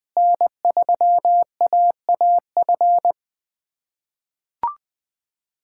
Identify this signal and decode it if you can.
Morse code, heard as a keyed tone, about 20 wpm, 700 Hz. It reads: N3AAF
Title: USA Callsign DE Callsign - 5 Characters - 20wpm